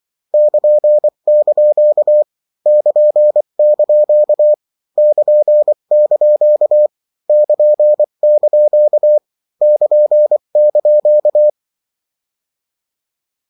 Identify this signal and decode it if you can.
Morse code, heard as a keyed tone, about 24 wpm, 600 Hz. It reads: <KN>) <KN>) <KN>) <KN>) <KN>)
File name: morse mp3 / lecon16-vitesse 22